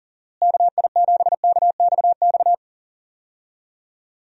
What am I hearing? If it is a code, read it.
Morse code, heard as a keyed tone, about 40 wpm, 700 Hz. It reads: KI7KXX
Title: USA Call Signs - 6 Characters - Repeated 3x - 40wpm